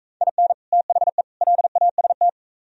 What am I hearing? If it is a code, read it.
Morse code, heard as a keyed tone, about 43 wpm, 700 Hz. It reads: IN THE LAST